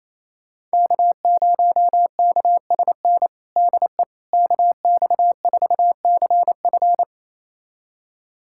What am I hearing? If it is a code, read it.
Morse code, heard as a keyed tone, about 28 wpm, 700 Hz. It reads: K0KSN DE KX4CF
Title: USA Callsign DE Callsign 28wpm